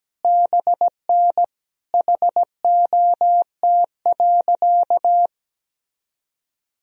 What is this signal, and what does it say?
Morse code, heard as a keyed tone, about 17 wpm, 700 Hz. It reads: BN HOT.
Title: Sentences from Top 500 Words and Abbreviations 17wpm